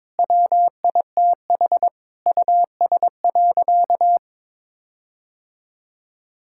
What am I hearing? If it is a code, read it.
Morse code, heard as a keyed tone, about 22 wpm, 700 Hz. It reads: WITH US.
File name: practice / 22wpm / Sentences with 8-Word Phrases - 22wpm